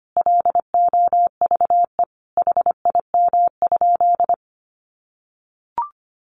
Morse code, heard as a keyed tone, about 25 wpm, 700 Hz. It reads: LOVE HIM?
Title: Sentences from Top 400 Words 25wpm